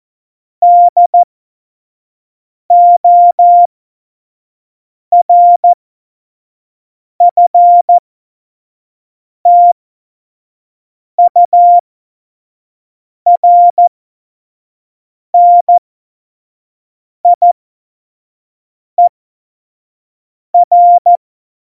Morse code, heard as a keyed tone, about 14 wpm, spaced as for 5 wpm, 700 Hz. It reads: DORFTURNIER
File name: Morsekurs / Witze / Witz 15. September 2020, 25 F70